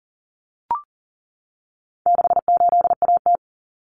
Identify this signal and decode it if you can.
Morse code, heard as a keyed tone, about 40 wpm, 700 Hz. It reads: 68AT
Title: Common Cut Numbers - 4-Digit 40wpm